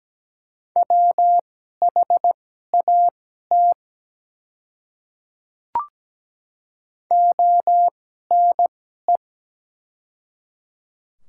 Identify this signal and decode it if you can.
Morse code, heard as a keyed tone, about 17 wpm, spaced as for 12 wpm, 700 Hz. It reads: WHAT ONE
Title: 2x Character Spacing - Top 100 Words - 17wpm